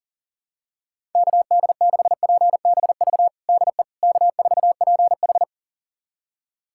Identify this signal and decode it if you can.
Morse code, heard as a keyed tone, about 37 wpm, 700 Hz. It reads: KD6PBV DE K4PH
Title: USA Callsign DE Callsign 40wpm